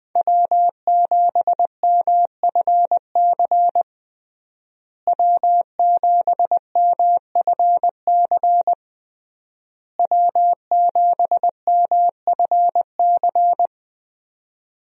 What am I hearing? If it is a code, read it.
Morse code, heard as a keyed tone, about 20 wpm, 700 Hz. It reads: W7MFC W7MFC W7MFC
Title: USA Call Signs - 5 Characters - Repeated 3x - 20wpm